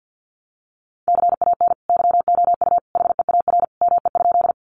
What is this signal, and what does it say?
Morse code, heard as a keyed tone, about 50 wpm, 700 Hz. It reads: CAN YOU HEAR ME?
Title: Sentences from Top 300 Words 50wpm